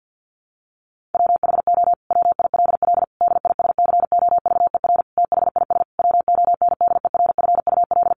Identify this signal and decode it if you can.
Morse code, heard as a keyed tone, about 50 wpm, 700 Hz. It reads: WHO WILL DISCOVER THIS WONDERFUL
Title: Sentences from Top 1100 Words 50wpm